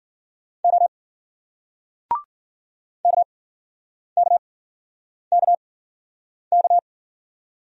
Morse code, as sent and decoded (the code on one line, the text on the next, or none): --- / -.- -.- -.- -.-
O KKKK